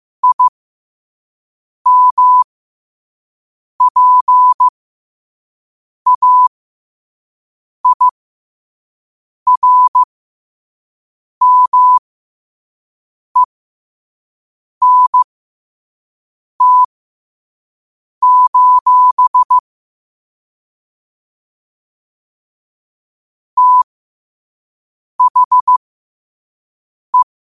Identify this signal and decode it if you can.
Morse code, heard as a keyed tone, about 15 wpm, spaced as for 5 wpm, 1000 Hz. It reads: IMPAIRMENT: THE
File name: 5wpm